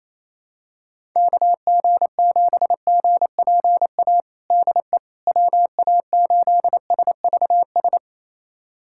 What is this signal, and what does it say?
Morse code, heard as a keyed tone, about 27 wpm, 700 Hz. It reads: KG7GPA DE WA8SVS
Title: USA Callsign DE Callsign 28wpm